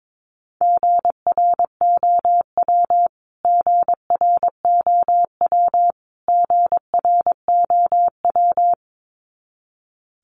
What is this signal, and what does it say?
Morse code, heard as a keyed tone, about 22 wpm, 700 Hz. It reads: GROW GROW GROW
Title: Top 500 Words Repeated 22wpm